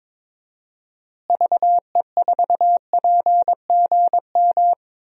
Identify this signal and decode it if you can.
Morse code, heard as a keyed tone, about 21 wpm, 700 Hz. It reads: VE4PGM
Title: DXCC-100 Call Signs - 6 Characters - Repeated 2x - 22wpm